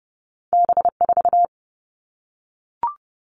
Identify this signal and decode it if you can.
Morse code, heard as a keyed tone, about 30 wpm, 700 Hz. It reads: B4
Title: All Cut Numbers - 2-Digit 30wpm